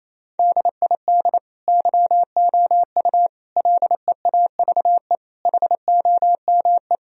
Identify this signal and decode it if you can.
Morse code, heard as a keyed tone, about 28 wpm, 700 Hz. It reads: DID YOU LEAVE HOME
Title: Sentences from Top 1000 Words - Review Part 4 - 28wpm